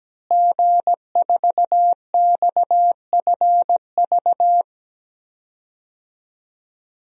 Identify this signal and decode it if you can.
Morse code, heard as a keyed tone, about 17 wpm, 700 Hz. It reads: G4XFV
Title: DXCC-100 Call Signs - 5 Characters - Repeated 3x - 17wpm